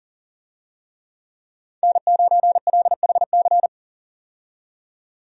Morse code, heard as a keyed tone, about 40 wpm, 700 Hz. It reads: N9LHC
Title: USA Call Signs - 5 Characters - 40wpm